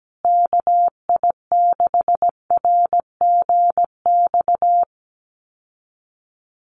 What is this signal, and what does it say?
Morse code, heard as a keyed tone, about 17 wpm, 700 Hz. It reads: KI6RGX
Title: USA Call Signs Repeated 2x 17wpm